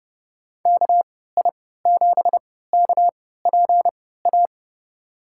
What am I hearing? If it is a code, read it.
Morse code, heard as a keyed tone, about 30 wpm, spaced as for 17 wpm, 700 Hz. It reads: KI7KPA
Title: USA Call Signs - 6 Characters - 3x Character Spacing - 30wpm